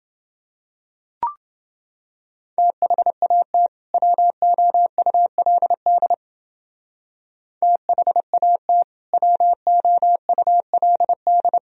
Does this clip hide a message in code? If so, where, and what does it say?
Yes: Morse code from 2.6 to 11.6 s, heard as a keyed tone, about 29 wpm, 700 Hz. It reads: THAT WOULD THAT WOULD